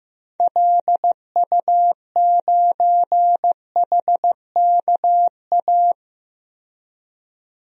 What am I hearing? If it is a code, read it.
Morse code, heard as a keyed tone, about 15 wpm, 700 Hz. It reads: LU9HKA
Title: DXCC-100 Call Signs - 6 Characters - 15wpm